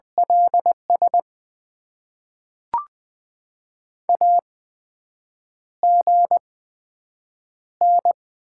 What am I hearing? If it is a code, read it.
Morse code, heard as a keyed tone, about 20 wpm, spaced as for 7 wpm, 700 Hz. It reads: LS AGN
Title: Top 90 QSO Elements - 8x Character Spacing - 20wpm